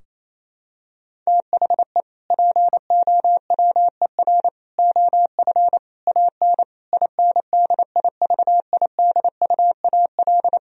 Morse code, heard as a keyed tone, about 28 wpm, 700 Hz. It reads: THE POWER OF AN INDIVIDUAL
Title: Sentences from Top 1000 Words - Review Part 4 - 28wpm